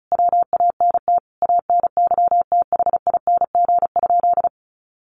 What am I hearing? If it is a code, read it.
Morse code, heard as a keyed tone, about 34 wpm, 700 Hz. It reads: WANT ANYTHING?
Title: Sentences from Top 1000 Words 35wpm